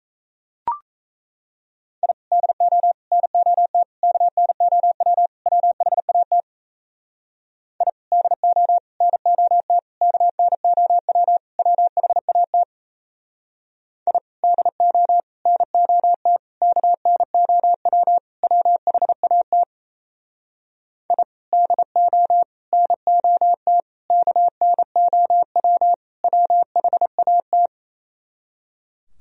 Morse code, heard as a keyed tone, about 37 wpm, 700 Hz. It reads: I DO NOT KNOW WHAT I DO NOT KNOW WHAT I DO NOT KNOW WHAT I DO NOT KNOW WHAT